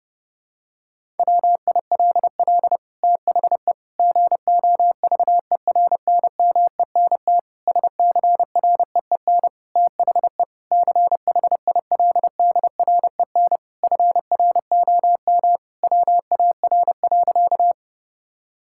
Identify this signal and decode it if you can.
Morse code, heard as a keyed tone, about 32 wpm, 700 Hz. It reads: WILL THE GOVERNMENT SCREEN THE CHILDREN FROM WAR.
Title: Sentences from Top 1000 Words - Review Part 6 - 30wpm